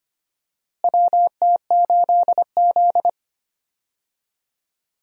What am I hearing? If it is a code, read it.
Morse code, heard as a keyed tone, about 25 wpm, 700 Hz. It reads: WT8Z
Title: USA Call Signs - 4 Characters - Repeated 3x - 25wpm